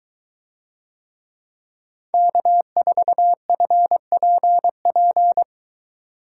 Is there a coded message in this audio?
Yes: Morse code from 2.1 to 5.4 s, heard as a keyed tone, about 23 wpm, 700 Hz. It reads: K4FPP